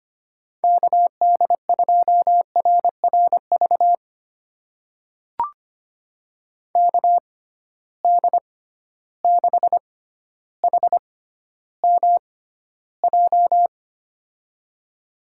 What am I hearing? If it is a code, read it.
Morse code, heard as a keyed tone, about 25 wpm, spaced as for 13 wpm, 700 Hz. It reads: KD2RRV KD6HMJ
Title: USA Call Signs - 6 Characters - 6x Character Spacing - 25wpm